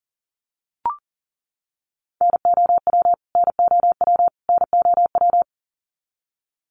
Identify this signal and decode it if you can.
Morse code, heard as a keyed tone, about 40 wpm, 700 Hz. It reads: NOW NOW NOW
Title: Top 200 Words Repeated 40wpm